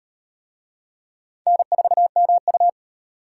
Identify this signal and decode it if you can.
Morse code, heard as a keyed tone, about 36 wpm, 700 Hz. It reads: N4MU